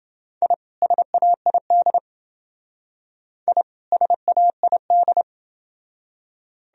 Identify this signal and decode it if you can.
Morse code, heard as a keyed tone, about 28 wpm, 700 Hz. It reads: I SAID I SAID